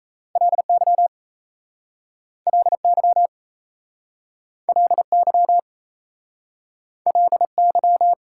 Morse code, heard as a keyed tone, about 36 wpm, 700 Hz. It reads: LY LY LY LY